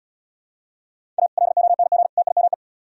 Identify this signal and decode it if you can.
Morse code, heard as a keyed tone, about 72 wpm, 700 Hz. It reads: A LONG TIME